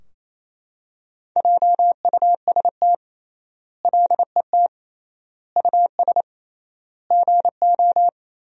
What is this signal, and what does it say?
Morse code, heard as a keyed tone, about 28 wpm, 700 Hz. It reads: JUST LET US GO
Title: 3x Word-Spacing - Sentences from Top 500 Words - 28wpm